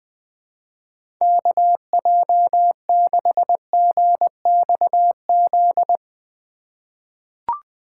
Morse code, heard as a keyed tone, about 21 wpm, 700 Hz. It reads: KJ6GXZ
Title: USA Call Signs - 6 Characters - 3x Character Spacing - 20wpm